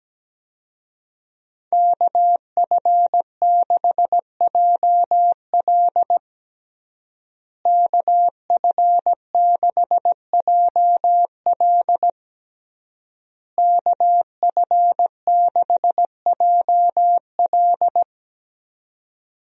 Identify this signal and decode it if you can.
Morse code, heard as a keyed tone, about 17 wpm, 700 Hz. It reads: KF6JL KF6JL KF6JL
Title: USA Call Signs - 5 Characters - Repeated 3x - 17wpm